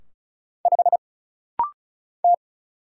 Morse code, heard as a keyed tone, about 36 wpm, 700 Hz. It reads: 5 T
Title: Single Letter-Number - Rapid-Fire 35wpm